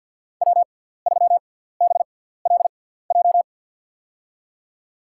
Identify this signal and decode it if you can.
Morse code, heard as a keyed tone, about 52 wpm, spaced as for 17 wpm, 700 Hz. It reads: W3BLJ
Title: USA Call Signs - 6x Character Spacing - 50wpm